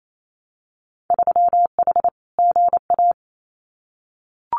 Morse code, heard as a keyed tone, about 28 wpm, 700 Hz. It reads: 3H GA
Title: Winter Field Day Exchanges Encore 28wpm